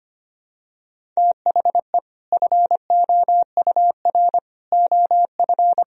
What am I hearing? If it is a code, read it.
Morse code, heard as a keyed tone, about 25 wpm, 700 Hz. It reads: THE FOUR OF
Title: Sentences from Top 1000 Words - Review Part 1 - 25wpm